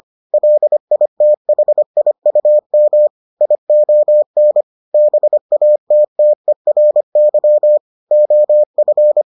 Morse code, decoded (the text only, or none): LITHIUM ION BATTERY OF